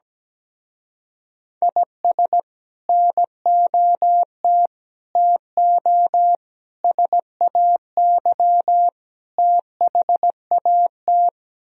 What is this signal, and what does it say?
Morse code, heard as a keyed tone, about 17 wpm, 700 Hz. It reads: IS NOT TO SAY THAT
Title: Speed-Racing - Sets of 5 Words 17wpm